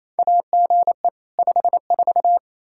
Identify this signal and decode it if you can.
Morse code, heard as a keyed tone, about 28 wpm, 700 Hz. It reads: AGE 54
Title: Practice QSO - Exchange 3 - More Info - 28wpm